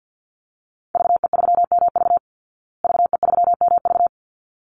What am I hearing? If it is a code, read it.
Morse code, heard as a keyed tone, about 50 wpm, 700 Hz. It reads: VE3MV VE3MV
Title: Call Signs Repeated 3x - 50wpm